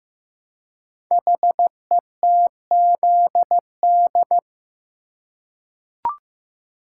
Morse code, heard as a keyed tone, about 15 wpm, 700 Hz. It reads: HETZD
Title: Code Groups - 5 Letters - 15wpm